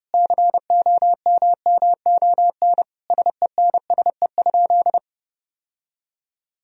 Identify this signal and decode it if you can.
Morse code, heard as a keyed tone, about 30 wpm, 700 Hz. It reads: COMMON SENSE?
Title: Sentences from Top 1000 Words - Review Part 4 - 30wpm